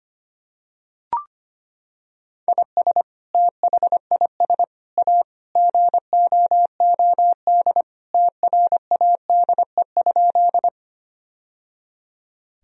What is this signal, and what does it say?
Morse code, heard as a keyed tone, about 25 wpm, 700 Hz. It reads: IS THIS A GOOD TRADE?
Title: Sentences from Top 1000 Words - Review Part 4 - 25wpm